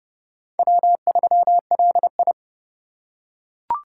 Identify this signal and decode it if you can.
Morse code, heard as a keyed tone, about 31 wpm, 700 Hz. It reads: W3LI
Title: USA Call Signs - 4 Characters - 3x Character Spacing - 30wpm